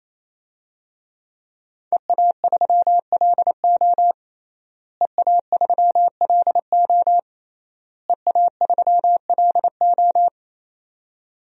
Morse code, heard as a keyed tone, about 28 wpm, 700 Hz. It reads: EA3LO EA3LO EA3LO
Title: DXCC-100 Call Signs - 5 Characters - Repeated 3x - 28wpm